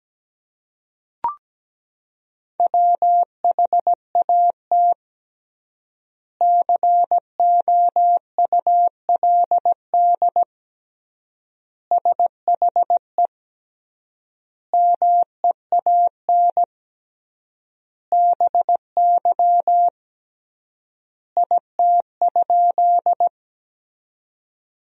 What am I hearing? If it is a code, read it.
Morse code, heard as a keyed tone, about 17 wpm, 700 Hz. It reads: WHAT COULD SHE MEAN BY IT?
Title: 3x Word-Spacing - Sentences from Top 200 Words - 17wpm